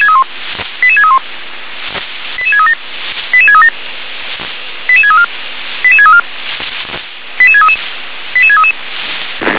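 Five-tone selective call (ZVEI), heard as quick runs of five tones, tones of about 70 ms. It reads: 88631 88637 88637 88634 88634 88633 88633
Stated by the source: ZVEI five-tone call received on SDR.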